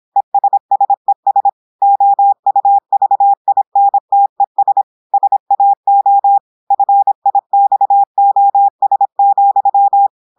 Morse code, heard as a keyed tone, about 26 wpm, 800 Hz. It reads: ESSES OUVINTES SAO FIXOS,
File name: portugais texte en clair 2 vitesse 26 mots minutes